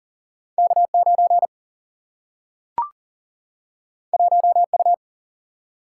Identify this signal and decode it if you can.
Morse code, heard as a keyed tone, about 39 wpm, 700 Hz. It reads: K 9 1 U
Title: Single-Letter Single-Number 40wpm